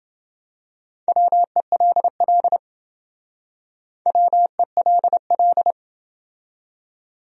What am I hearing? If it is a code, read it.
Morse code, heard as a keyed tone, about 28 wpm, 700 Hz. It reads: WELL WELL